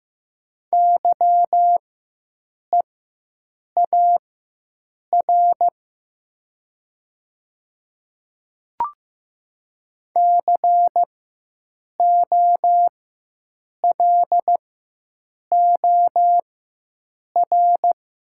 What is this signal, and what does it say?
Morse code, heard as a keyed tone, about 15 wpm, spaced as for 7 wpm, 700 Hz. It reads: YEAR COLOR
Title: 4x Character Spacing - Top 300 Words - 15wpm